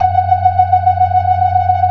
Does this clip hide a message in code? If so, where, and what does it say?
No.